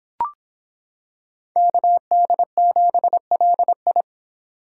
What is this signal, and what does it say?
Morse code, heard as a keyed tone, about 26 wpm, 700 Hz. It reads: KD7LI